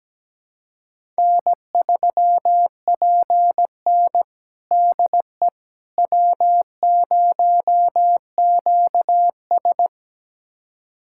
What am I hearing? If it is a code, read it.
Morse code, heard as a keyed tone, about 17 wpm, 700 Hz. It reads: N3PN DE W0QS
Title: USA Callsign DE Callsign - 4 Characters - 17wpm